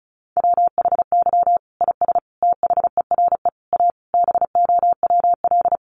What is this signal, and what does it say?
Morse code, heard as a keyed tone, about 34 wpm, 700 Hz. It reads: WHY IS THERE A BOWL
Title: Sentences from Top 1200 Words 35wpm